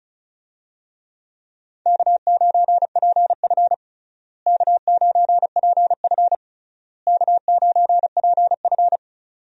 Morse code, heard as a keyed tone, about 34 wpm, 700 Hz. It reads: K9PF K9PF K9PF
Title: USA Call Signs - 4 Characters - Repeated 3x - 35wpm